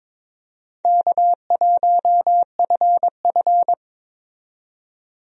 Morse code, heard as a keyed tone, about 22 wpm, 700 Hz. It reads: K1FF
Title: USA Call Signs - 4 Characters - 22wpm